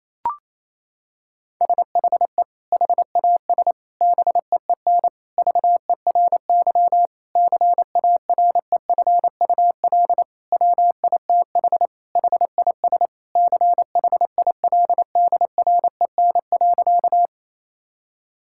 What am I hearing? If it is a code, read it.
Morse code, heard as a keyed tone, about 27 wpm, 700 Hz. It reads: SHE HAS BEEN VERY CAREFUL WITH HIS CHILDREN.
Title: Sentences from Top 1600 Words 28wpm